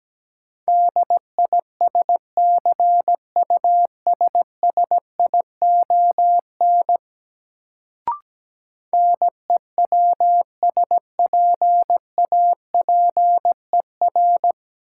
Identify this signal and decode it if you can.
Morse code, heard as a keyed tone, about 17 wpm, 700 Hz. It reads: DISCUSSION NEWSPAPER
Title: Sentences from Top 1100 Words - New Word Drill Alt - 17wpm